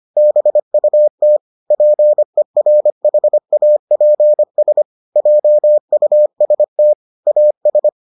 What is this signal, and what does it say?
Morse code, heard as a keyed tone, about 25 wpm, 600 Hz. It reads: BUT PERHAPS JUST AS